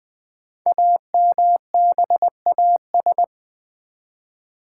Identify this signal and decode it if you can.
Morse code, heard as a keyed tone, about 20 wpm, 700 Hz. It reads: AMBAS
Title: Build A Word - 5 Letters At A Time - 10-Letter Words - 20wpm